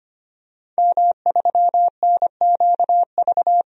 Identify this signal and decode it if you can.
Morse code, heard as a keyed tone, about 25 wpm, 700 Hz. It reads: M3NQV
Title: DXCC-100 Call Signs - 5 Characters - 25wpm